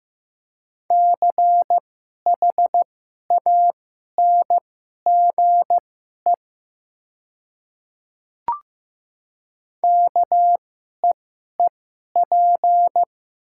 Morse code, heard as a keyed tone, about 15 wpm, spaced as for 11 wpm, 700 Hz. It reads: CHANGE KEEP
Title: 2x Character Spacing - Top 300 Words - 15wpm